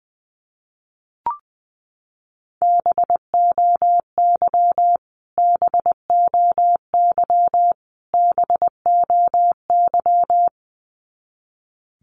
Morse code, heard as a keyed tone, about 20 wpm, 700 Hz. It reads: BOY BOY BOY
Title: Top 300 Words Repeated 20wpm